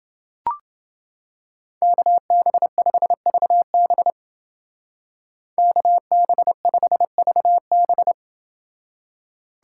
Morse code, heard as a keyed tone, about 28 wpm, 700 Hz. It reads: KB5VB KB5VB